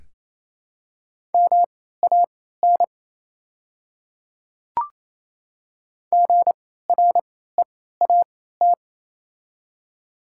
Morse code, heard as a keyed tone, about 28 wpm, spaced as for 16 wpm, 690 Hz. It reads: MAN GREAT
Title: 3x Character Spacing - Top 200 Words - 28wpm